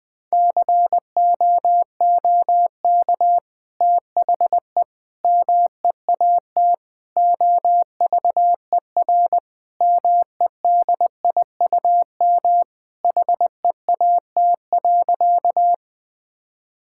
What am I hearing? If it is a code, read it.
Morse code, heard as a keyed tone, about 21 wpm, 700 Hz. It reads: COOK THE MEAT OVER MEDIUM HEAT.
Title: Sentences from Top 1000 Words - Review Part 6 - 20wpm